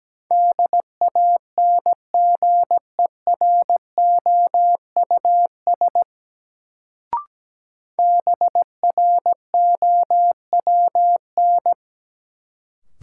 Morse code, heard as a keyed tone, about 17 wpm, 700 Hz. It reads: DANGEROUS BROWN
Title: Sentences from Top 1200 Words - New Word Drill - 17wpm